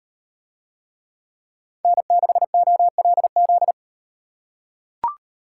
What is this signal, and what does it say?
Morse code, heard as a keyed tone, about 38 wpm, 700 Hz. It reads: N6OLZ